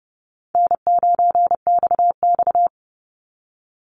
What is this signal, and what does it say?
Morse code, heard as a keyed tone, about 30 wpm, 700 Hz. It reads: N9XX
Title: Easy Call Signs Repeated 3x - 30wpm